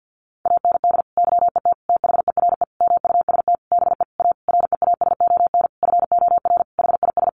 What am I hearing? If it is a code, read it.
Morse code, heard as a keyed tone, about 50 wpm, 700 Hz. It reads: AND YET THERE MUST BE A REASON FOR HIS
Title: Sentences from Top 1300 Words 50wpm